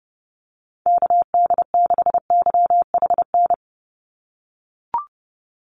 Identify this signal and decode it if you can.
Morse code, heard as a keyed tone, about 30 wpm, 700 Hz. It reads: KD6YHN
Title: USA Call Signs 30wpm